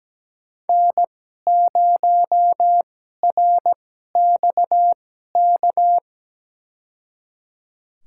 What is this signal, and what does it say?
Morse code, heard as a keyed tone, about 17 wpm, spaced as for 12 wpm, 700 Hz. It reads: N0RXK
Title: USA Call Signs - 5 Characters - 2x Character Spacing - 17wpm